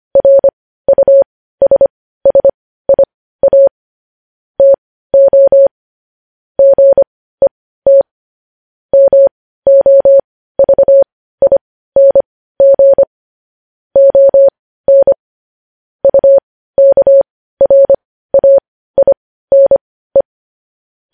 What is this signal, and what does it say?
Morse code, heard as a keyed tone, about 25 wpm, spaced as for 15 wpm, 550 Hz. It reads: RUSSIA TO GET MOVING ON UKRAINE
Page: news-headlines-in-morse-code-at-15-wpm